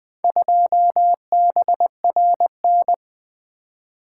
Morse code, heard as a keyed tone, about 20 wpm, 700 Hz. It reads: 2BRN
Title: Four Letter-Number Combinations - 20wpm